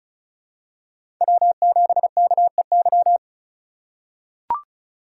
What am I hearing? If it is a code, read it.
Morse code, heard as a keyed tone, about 36 wpm, 700 Hz. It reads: W7KEY
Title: USA Call Signs - 5 Characters - 4x Character Spacing - 35wpm